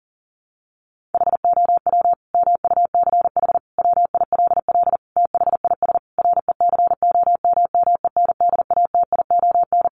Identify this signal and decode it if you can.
Morse code, heard as a keyed tone, about 40 wpm, 700 Hz. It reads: HOW MUCH WILL THIS RECOMMENDATION
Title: Sentences from Top 1400 Words 40wpm